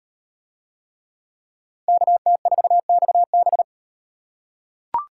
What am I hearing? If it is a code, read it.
Morse code, heard as a keyed tone, about 37 wpm, 700 Hz. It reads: KT4XB